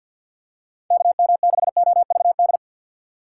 Morse code, heard as a keyed tone, about 50 wpm, 700 Hz. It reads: KM6OVD
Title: USA Call Signs - 6 Characters - 3x Character Spacing - 50wpm